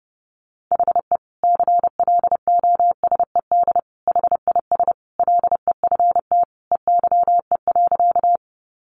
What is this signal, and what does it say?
Morse code, heard as a keyed tone, about 30 wpm, 700 Hz. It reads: HE CLOSED HIS LEFT EYE.
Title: Sentences from Top 1400 Words 30wpm